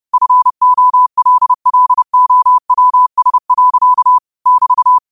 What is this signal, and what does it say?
Morse code, heard as a keyed tone, about 30 wpm, 1000 Hz. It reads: FOLLOWS. <BT>
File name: qotdfiles / 30wpm